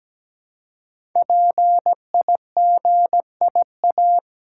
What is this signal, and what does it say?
Morse code, heard as a keyed tone, about 17 wpm, 700 Hz. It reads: PIGIA
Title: Code Groups - 5 Letters - 17wpm